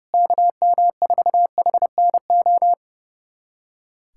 Morse code, heard as a keyed tone, about 32 wpm, 700 Hz. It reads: KM4HNO
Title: USA Callsign DE Callsign - 6 Characters - 30wpm